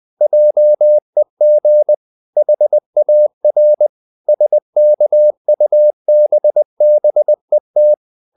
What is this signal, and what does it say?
Morse code, heard as a keyed tone, about 20 wpm, 600 Hz. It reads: JEG HAR SKUBBET
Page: Ferie 20 wpm